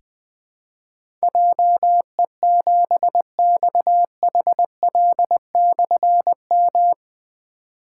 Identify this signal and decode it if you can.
Morse code, heard as a keyed tone, about 20 wpm, 700 Hz. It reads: JE7XHL/M
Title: Difficult Call Signs 20wpm